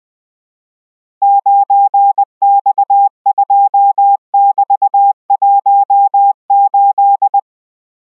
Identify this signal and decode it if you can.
Morse code, heard as a keyed tone, about 20 wpm, 800 Hz. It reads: 9X2<BT>18